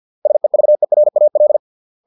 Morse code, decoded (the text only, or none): SEVERAL